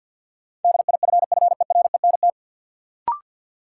Morse code, heard as a keyed tone, about 53 wpm, 700 Hz. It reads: DIFFERENT
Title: Sentences from Top 600 Words - New Word Drill - 50wpm